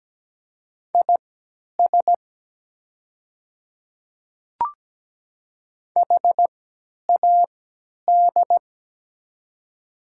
Morse code, heard as a keyed tone, about 17 wpm, spaced as for 10 wpm, 700 Hz. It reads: IS HAD
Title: 3x Character Spacing - Top 200 Words - 17wpm